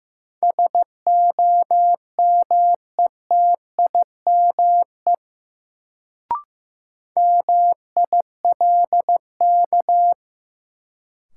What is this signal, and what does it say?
Morse code, heard as a keyed tone, about 15 wpm, 700 Hz. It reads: SOMETIME MILK
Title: Sentences from Top 1500 Words - New Word Drill Alt - 15wpm